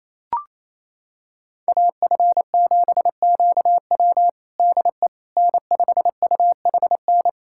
Morse code, heard as a keyed tone, about 28 wpm, 700 Hz. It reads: AF7QW DE N5UHN
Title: USA Callsign DE Callsign - 5 Characters - 28wpm